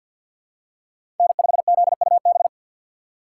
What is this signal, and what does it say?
Morse code, heard as a keyed tone, about 50 wpm, 700 Hz. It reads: N5ZUB
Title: USA Call Signs - 5 Characters - 6x Character Spacing - 50wpm